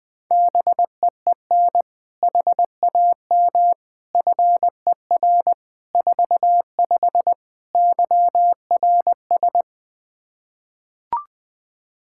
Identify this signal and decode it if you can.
Morse code, heard as a keyed tone, about 20 wpm, 700 Hz. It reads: BEEN HAM FER 45 YRS
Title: Practice QSO - Exchange 3 - More Info - 20wpm